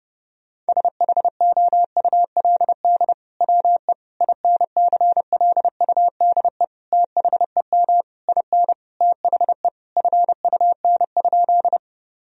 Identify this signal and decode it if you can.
Morse code, heard as a keyed tone, about 30 wpm, 700 Hz. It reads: SHOULD WE INCLUDE THEM IN THE FUN?
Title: Sentences from Top 1000 Words - Review Part 3 - 30wpm